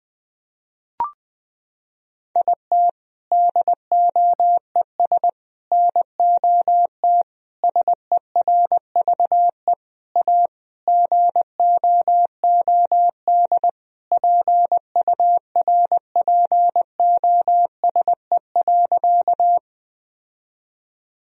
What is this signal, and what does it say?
Morse code, heard as a keyed tone, about 20 wpm, 700 Hz. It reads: IT DOES NOT SERVE A GOOD PURPOSE.
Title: Sentences from Top 1000 Words - Review Part 2 - 20wpm